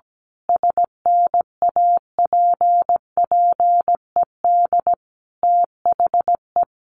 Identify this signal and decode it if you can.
Morse code, heard as a keyed tone, about 17 wpm, 700 Hz. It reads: SNAPPED THE
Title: A Sound Of Thunder 17wpm